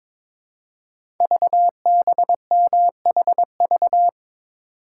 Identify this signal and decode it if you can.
Morse code, heard as a keyed tone, about 22 wpm, 700 Hz. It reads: VBMHV
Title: Code Groups - 5 Letters - 22wpm